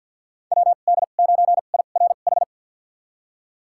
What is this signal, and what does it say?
Morse code, heard as a keyed tone, about 53 wpm, spaced as for 37 wpm, 700 Hz. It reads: WD9IRH
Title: USA Call Signs - 2x Character Spacing - 50wpm